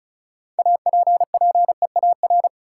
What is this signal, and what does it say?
Morse code, heard as a keyed tone, about 36 wpm, 700 Hz. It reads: APPEAR
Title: Sentences from Top 1200 Words - New Word Drill Alt - 35wpm